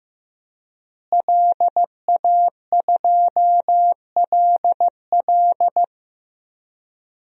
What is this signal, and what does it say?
Morse code, heard as a keyed tone, about 16 wpm, 700 Hz. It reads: LA2LL
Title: DXCC-100 Call Signs - 5 Characters - Repeated 2x - 15wpm